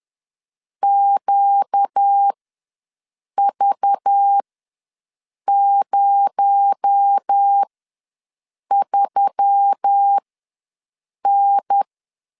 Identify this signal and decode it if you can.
Morse code, heard as a keyed tone, about 10 wpm, spaced as for 6 wpm, 800 Hz. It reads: QV03N
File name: LEZIONI / 7 GF9Z / 7 GF ws 10wpm 3918